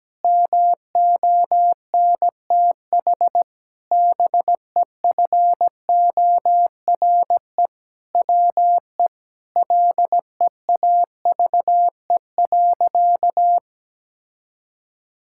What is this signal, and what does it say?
Morse code, heard as a keyed tone, about 17 wpm, 700 Hz. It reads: MONTH BEFORE WE LEAVE.
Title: Sentences from Top 1000 Words - Review Part 5 - 17wpm